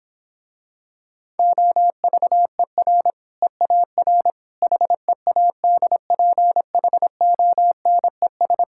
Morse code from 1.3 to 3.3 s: OVER

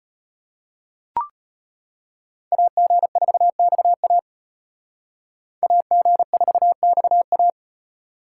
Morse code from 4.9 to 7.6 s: AG4XA

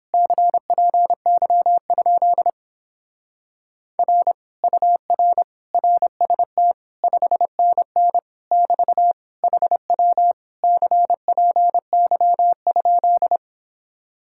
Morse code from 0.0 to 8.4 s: CPY? R UR RST 5NN

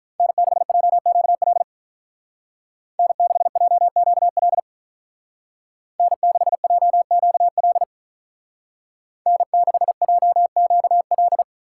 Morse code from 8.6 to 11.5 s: N6JQL

